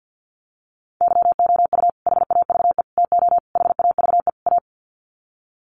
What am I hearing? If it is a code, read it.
Morse code, heard as a keyed tone, about 50 wpm, 700 Hz. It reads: YOU HAVE TO HAVE A
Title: Sets of 5 Words Encore 50wpm